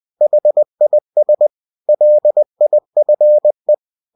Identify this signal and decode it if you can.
Morse code, heard as a keyed tone, about 20 wpm, 600 Hz. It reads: HIS LIFE
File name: cw-books / winnie-the-pooh / winnie-20wpm-page-0023